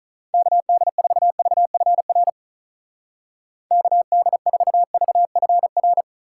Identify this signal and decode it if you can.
Morse code, heard as a keyed tone, about 42 wpm, 700 Hz. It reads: KD4VFR KD4VFR
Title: Speed Racing - USA 2x3 Call Signs 35wpm